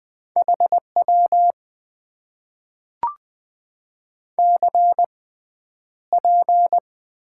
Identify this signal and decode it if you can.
Morse code, heard as a keyed tone, about 20 wpm, spaced as for 10 wpm, 700 Hz. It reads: HW CP
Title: Top 90 QSO Elements - 6x Character Spacing - 20wpm